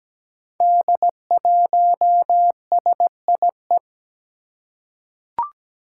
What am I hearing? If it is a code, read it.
Morse code, heard as a keyed tone, about 17 wpm, 700 Hz. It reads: D1SIE
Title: Five Challenging Letter-Number Combinations 17wpm